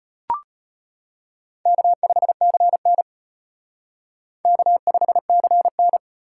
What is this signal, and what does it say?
Morse code, heard as a keyed tone, about 32 wpm, 700 Hz. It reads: K5CN K5CN